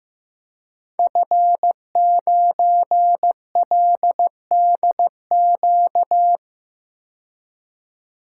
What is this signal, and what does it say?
Morse code, heard as a keyed tone, about 15 wpm, 700 Hz. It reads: F9LDQ
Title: Code Groups - 5 Letter:Numbers - 15wpm